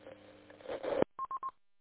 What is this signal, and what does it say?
There is no code.